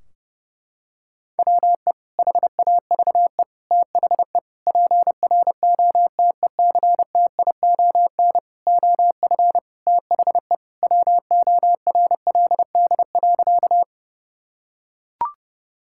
Morse code, heard as a keyed tone, about 32 wpm, 700 Hz. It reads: WE HAVE THE PROTECTION OF THE WORLD.
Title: Sentences from Top 1600 Words 30wpm